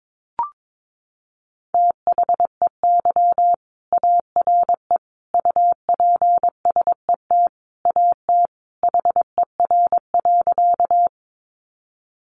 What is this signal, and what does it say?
Morse code, heard as a keyed tone, about 22 wpm, 700 Hz. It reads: THEY ARE UPSET AT HER.
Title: Sentences from Top 800 Words 22wpm